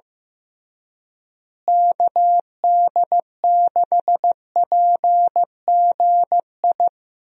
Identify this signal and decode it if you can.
Morse code, heard as a keyed tone, about 15 wpm, 700 Hz. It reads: KD6PGI
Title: USA Call Signs - 6 Characters - Repeated 3x - 15wpm